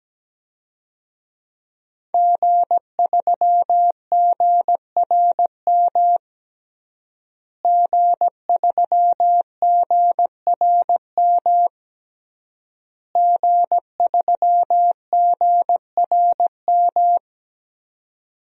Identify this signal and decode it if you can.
Morse code, heard as a keyed tone, about 17 wpm, 700 Hz. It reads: G3GRM G3GRM G3GRM
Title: DXCC-100 Call Signs - 5 Characters - Repeated 3x - 17wpm